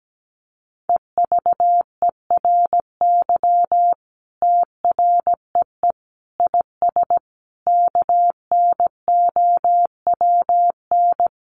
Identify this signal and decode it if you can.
Morse code, heard as a keyed tone, about 17 wpm, 700 Hz. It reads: EVERY TREE IS KNOWN
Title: Sentences from Top 1300 Words 17wpm